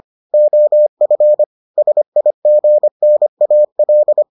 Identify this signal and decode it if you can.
Morse code, heard as a keyed tone, about 25 wpm, 600 Hz. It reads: OF SIGNAL